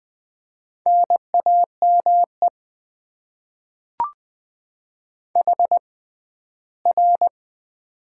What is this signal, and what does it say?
Morse code, heard as a keyed tone, about 21 wpm, spaced as for 14 wpm, 700 Hz. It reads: NAME HR
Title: Top 90 QSO Elements - 6x Character Spacing - 20wpm